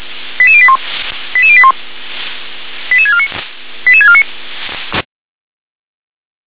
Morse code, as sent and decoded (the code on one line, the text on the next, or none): none